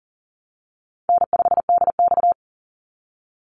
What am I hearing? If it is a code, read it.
Morse code, heard as a keyed tone, about 40 wpm, 700 Hz. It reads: N5DX
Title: Easy Call Signs Repeated 2x - 40wpm